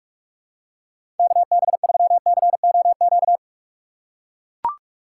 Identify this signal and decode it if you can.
Morse code, heard as a keyed tone, about 41 wpm, 700 Hz. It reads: KB3COQ
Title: USA Call Signs - 6 Characters - 3x Character Spacing - 45wpm